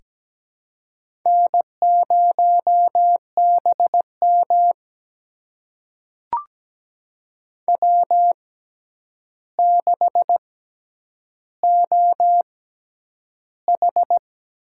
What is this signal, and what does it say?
Morse code, heard as a keyed tone, about 17 wpm, spaced as for 9 wpm, 700 Hz. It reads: N0BM W6OH
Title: USA Call Signs - 4 Characters - 6x Character Spacing - 17wpm